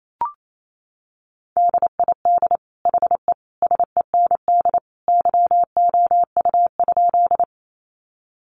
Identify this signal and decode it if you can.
Morse code, heard as a keyed tone, about 28 wpm, 700 Hz. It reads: DID HE SEND YOU?
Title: Sentences from Top 1500 Words 28wpm